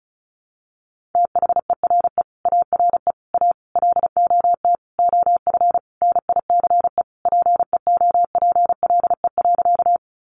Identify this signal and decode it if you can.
Morse code, heard as a keyed tone, about 34 wpm, 700 Hz. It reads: THERE ARE A LOT OF NICE PEOPLE.
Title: Sentences from Top 1100 Words 35wpm